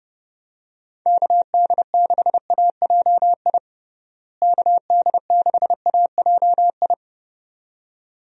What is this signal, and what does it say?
Morse code, heard as a keyed tone, about 30 wpm, 700 Hz. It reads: KD6AJI KD6AJI
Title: USA Call Signs - 6 Characters - Repeated 2x - 30wpm